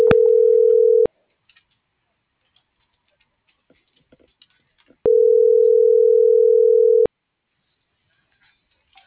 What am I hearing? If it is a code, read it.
Ringback tone (North America), 440 + 480 Hz.